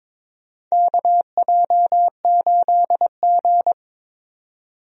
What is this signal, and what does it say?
Morse code, heard as a keyed tone, about 22 wpm, 700 Hz. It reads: KJ8G
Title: USA Call Signs - 4 Characters - 22wpm